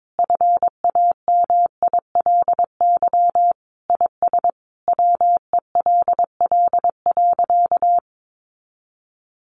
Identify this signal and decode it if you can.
Morse code, heard as a keyed tone, about 22 wpm, 700 Hz. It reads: FAMILY IS WELL.
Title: Sentences from Top 800 Words 22wpm